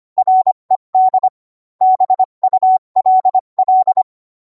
Morse code, heard as a keyed tone, about 25 wpm, 750 Hz. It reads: RED BULL